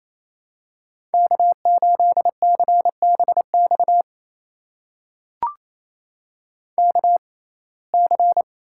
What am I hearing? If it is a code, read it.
Morse code, heard as a keyed tone, about 28 wpm, spaced as for 20 wpm, 700 Hz. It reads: K8CBX KC
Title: USA Call Signs - 5 Characters - 6x Character Spacing - 28wpm